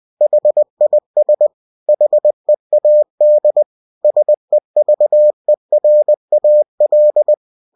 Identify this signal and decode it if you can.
Morse code, heard as a keyed tone, about 20 wpm, 600 Hz. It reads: HIS HEAD SEVERAL
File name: cw-books / winnie-the-pooh / winnie-20wpm-page-0034